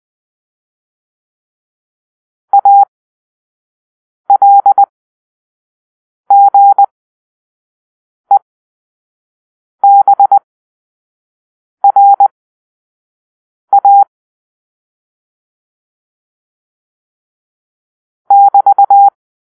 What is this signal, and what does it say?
Morse code, heard as a keyed tone, about 20 wpm, spaced as for 5 wpm, 800 Hz. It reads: ALGEBRA <BT>
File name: arrl / Grp-4 Drill-1